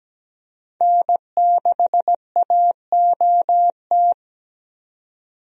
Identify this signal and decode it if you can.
Morse code, heard as a keyed tone, about 17 wpm, 700 Hz. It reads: N6AOT
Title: USA Call Signs - 5 Characters - 17wpm